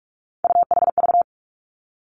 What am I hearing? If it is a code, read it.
Morse code, heard as a keyed tone, about 46 wpm, 700 Hz. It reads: UHV
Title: Three Challenging Letter-Number Combinations 45wpm